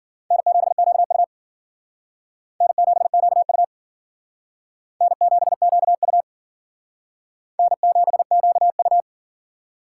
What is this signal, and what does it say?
Morse code, heard as a keyed tone, about 52 wpm, 700 Hz. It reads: N7QU N7QU N7QU N7QU